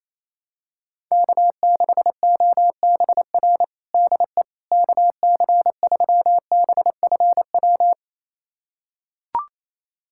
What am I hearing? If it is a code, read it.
Morse code, heard as a keyed tone, about 28 wpm, 700 Hz. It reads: K6OBR DE KC3BFW
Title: USA Callsign DE Callsign 28wpm